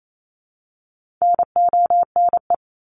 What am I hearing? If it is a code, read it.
Morse code, heard as a keyed tone, about 28 wpm, 700 Hz. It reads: NONE
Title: Top 1501-1750 Words 28wpm